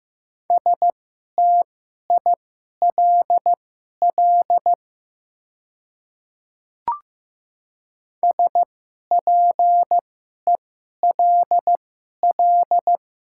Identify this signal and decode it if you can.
Morse code, heard as a keyed tone, about 15 wpm, spaced as for 11 wpm, 690 Hz. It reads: STILL SPELL
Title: 2x Character Spacing - Top 300 Words - 15wpm